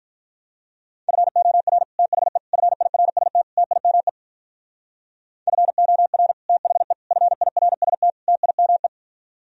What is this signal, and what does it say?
Morse code, heard as a keyed tone, about 43 wpm, 700 Hz. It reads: FOR THE FIRST TIME FOR THE FIRST TIME
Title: Speed-Racing - Sets of 4 Words 35wpm